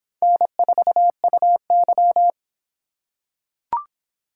Morse code, heard as a keyed tone, about 23 wpm, 700 Hz. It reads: N4UY